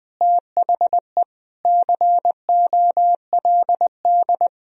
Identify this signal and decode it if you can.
Morse code, heard as a keyed tone, about 20 wpm, 700 Hz. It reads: THE COLD
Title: Speed-Racing - Sets of 5 Words 17wpm